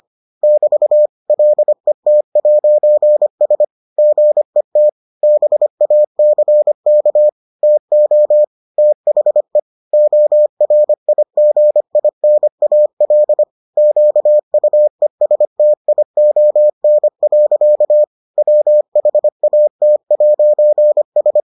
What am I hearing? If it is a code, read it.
Morse code, heard as a keyed tone, about 25 wpm, 600 Hz. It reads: <BT> LET'S GET BACK TO THE ORIGINAL QUESTION. WHAT'S